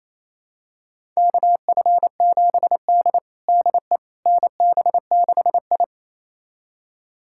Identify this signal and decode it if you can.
Morse code, heard as a keyed tone, about 28 wpm, 700 Hz. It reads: KF7D DE NB6I
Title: USA Callsign DE Callsign - 4 Characters - 28wpm